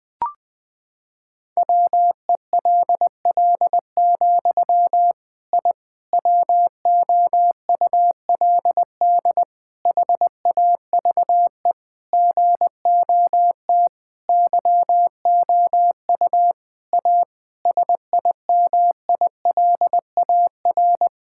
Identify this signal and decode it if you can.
Morse code, heard as a keyed tone, about 20 wpm, 700 Hz. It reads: WELL, I WOULD HAVE GOT YOU A SIMILAR
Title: Sentences from Top 1000 Words - Review Part 5 - 20wpm